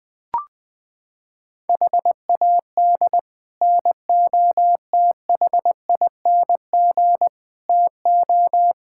Morse code, heard as a keyed tone, about 21 wpm, 700 Hz. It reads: HAD NOTHING TO
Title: Sets of 6 Words - 20wpm